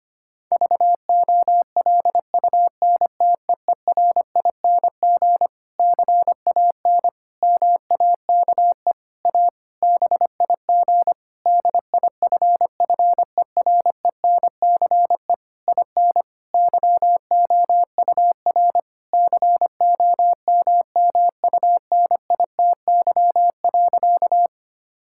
Morse code, heard as a keyed tone, about 25 wpm, 700 Hz. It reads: VOLUNTEERING CAN MAKE A BIG DIFFERENCE IN YOUR COMMUNITY.
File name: practice / 25wpm / Difficult Sentences with Hints - 25wpm